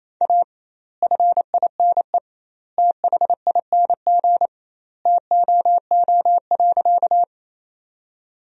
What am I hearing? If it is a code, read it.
Morse code, heard as a keyed tone, about 28 wpm, 700 Hz. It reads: A FINE THING TOO.
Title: 2x Word-Spacing - Sentences from Top 500 Words - 28wpm